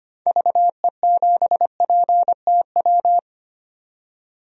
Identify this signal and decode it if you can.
Morse code, heard as a keyed tone, about 25 wpm, 700 Hz. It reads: VE7PTW
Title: DXCC-100 Call Signs - 6 Characters - 25wpm